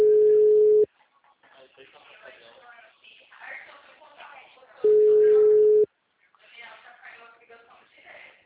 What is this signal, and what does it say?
Ringback tone (Europe, 425 Hz), with speech.
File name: out- -2020- - - .75319